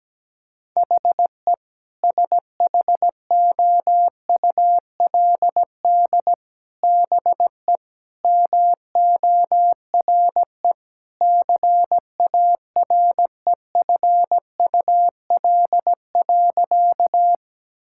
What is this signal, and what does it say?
Morse code, heard as a keyed tone, about 17 wpm, 700 Hz. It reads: HE SHOULD BE MORE CAREFUL.
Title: Sentences from Top 1600 Words 17wpm